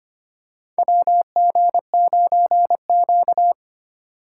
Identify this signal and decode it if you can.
Morse code, heard as a keyed tone, about 25 wpm, 700 Hz. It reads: WG9Q
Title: USA Call Signs - 4 Characters - 2x Character Spacing - 25wpm